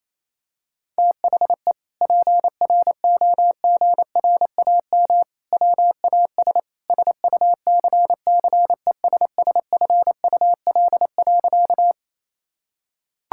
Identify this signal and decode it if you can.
Morse code, heard as a keyed tone, about 28 wpm, 700 Hz. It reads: THE PROGRAM WAS SUCCESSFUL.
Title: Sentences from Top 1000 Words - Review Part 5 - 28wpm